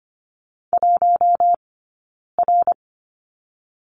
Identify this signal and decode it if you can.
Morse code, heard as a keyed tone, about 26 wpm, 700 Hz. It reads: 1 R
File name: Lesson 15d - Number 5 - Two Characters 25wpm